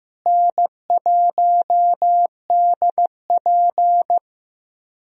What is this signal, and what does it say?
Morse code, heard as a keyed tone, about 15 wpm, 700 Hz. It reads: N1DP
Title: USA Call Signs - 4 Characters - Repeated 2x - 15wpm